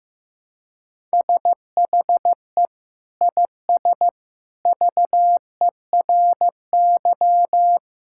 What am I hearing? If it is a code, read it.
Morse code, heard as a keyed tone, about 15 wpm, 700 Hz. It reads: SHE IS VERY
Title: Sentences from Top 1000 Words - Review Part 6 - 15wpm